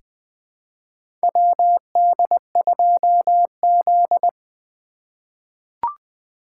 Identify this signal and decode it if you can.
Morse code, heard as a keyed tone, about 20 wpm, 700 Hz. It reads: WD2Z
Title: USA Call Signs - 4 Characters - 8x Character Spacing - 20wpm